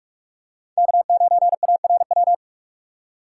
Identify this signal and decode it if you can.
Morse code, heard as a keyed tone, about 46 wpm, 700 Hz. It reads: K9ARW
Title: USA Call Signs - 5 Characters - 6x Character Spacing - 45wpm